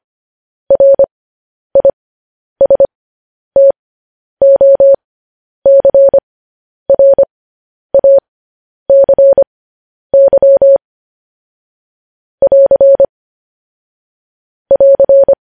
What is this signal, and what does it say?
Morse code, heard as a keyed tone, about 25 wpm, spaced as for 10 wpm, 550 Hz. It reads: RISTOCRACY <AR> <AR>
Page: news headlines in morse code at 10 wpm